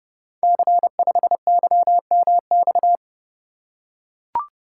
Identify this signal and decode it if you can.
Morse code, heard as a keyed tone, about 30 wpm, 700 Hz. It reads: C5YMX
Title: Code Groups - 5 Letter:Numbers - 30wpm